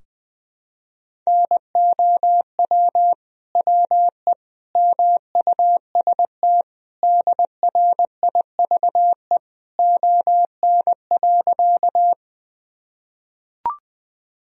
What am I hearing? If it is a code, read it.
Morse code, heard as a keyed tone, about 20 wpm, 700 Hz. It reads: NOW WE MUST DRIVE ON.
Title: Sentences from Top 1000 Words - Review Part 2 - 20wpm